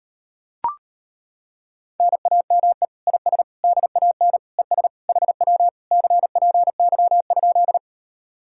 Morse code, heard as a keyed tone, about 33 wpm, 700 Hz. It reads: NAME IS DAN ES HW CPY?